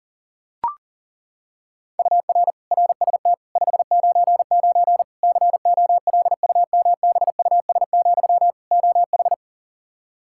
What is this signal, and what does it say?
Morse code, heard as a keyed tone, about 40 wpm, 700 Hz. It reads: UR RST 599 COLUMBUS, OH
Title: K1USN SST QSO Practice 40wpm